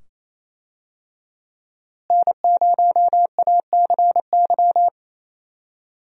Morse code, heard as a keyed tone, about 28 wpm, 700 Hz. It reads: N0ACY
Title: USA Call Signs - 5 Characters - Repeated 2x - 28wpm